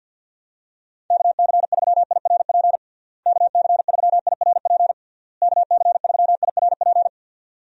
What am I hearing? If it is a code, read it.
Morse code, heard as a keyed tone, about 49 wpm, 700 Hz. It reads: KC3IRP KC3IRP KC3IRP
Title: USA Call Signs - 6 Characters - Repeated 3x - 50wpm